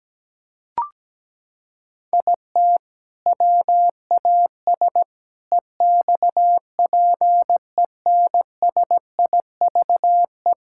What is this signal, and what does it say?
Morse code, heard as a keyed tone, about 17 wpm, 700 Hz. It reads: IT WAS EXPENSIVE
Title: Sentences from Top 1000 Words - Review Part 6 - 17wpm